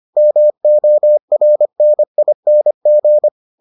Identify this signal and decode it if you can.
Morse code, heard as a keyed tone, about 25 wpm, 600 Hz. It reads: MORNING